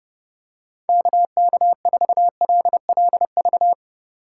Morse code, heard as a keyed tone, about 32 wpm, 700 Hz. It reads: KK4LLV
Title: USA Call Signs - 6 Characters - 3x Character Spacing - 30wpm